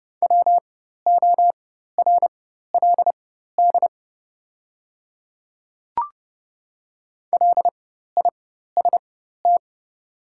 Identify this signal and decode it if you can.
Morse code, heard as a keyed tone, about 30 wpm, spaced as for 14 wpm, 700 Hz. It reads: WORLD LIST
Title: 4x Character Spacing - Top 500 Words - 30wpm